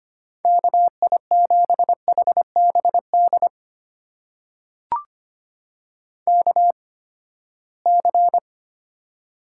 Morse code, heard as a keyed tone, about 25 wpm, spaced as for 17 wpm, 700 Hz. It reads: KI7HBD KC
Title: USA Call Signs - 6 Characters - 8x Character Spacing - 25wpm